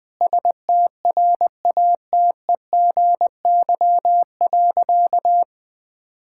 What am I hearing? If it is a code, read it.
Morse code, heard as a keyed tone, about 20 wpm, 700 Hz. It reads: STRATEGY.
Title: Sentences from Top 1000 Words - Review Part 5 - 20wpm